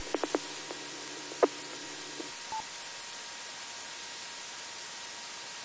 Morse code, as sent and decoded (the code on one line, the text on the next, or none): none